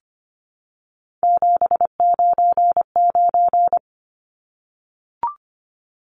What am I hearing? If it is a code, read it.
Morse code, heard as a keyed tone, about 25 wpm, 700 Hz. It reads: 799
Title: Numbers - 3-Digit 25wpm